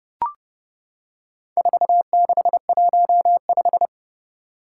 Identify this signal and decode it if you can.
Morse code, heard as a keyed tone, about 29 wpm, 700 Hz. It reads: 4615
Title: Four Challenging Letter-Number Combinations 30wpm